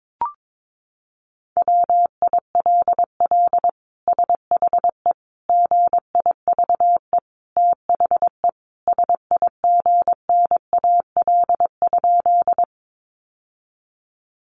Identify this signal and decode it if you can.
Morse code, heard as a keyed tone, about 22 wpm, 700 Hz. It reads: WILL SHE GIVE THE SIGNAL?
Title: Sentences from Top 1400 Words 22wpm